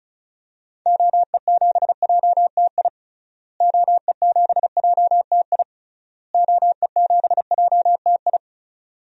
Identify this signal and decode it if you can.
Morse code, heard as a keyed tone, about 34 wpm, 700 Hz. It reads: OE7JTI OE7JTI OE7JTI
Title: DXCC-100 Call Signs - 6 Characters - Repeated 3x - 35wpm